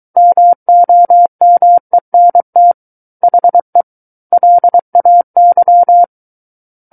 Morse code, heard as a keyed tone, about 23 wpm, 700 Hz. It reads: MOMENT HE LAY
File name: Book One / 14-Chapter 14 - 23 WPM